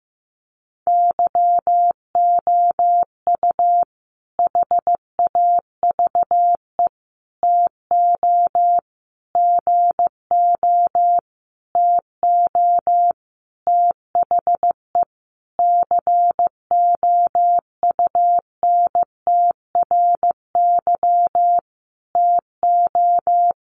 Morse code, heard as a keyed tone, about 15 wpm, 700 Hz. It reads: YOU HAVE TO GO TO THE COUNTRY TO
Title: Sentences from Top 1300 Words 15wpm